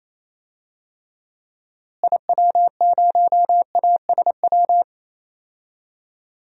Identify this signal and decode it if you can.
Morse code, heard as a keyed tone, about 27 wpm, 700 Hz. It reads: IW0ASW
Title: DXCC-100 Call Signs - 6 Characters - Repeated 2x - 28wpm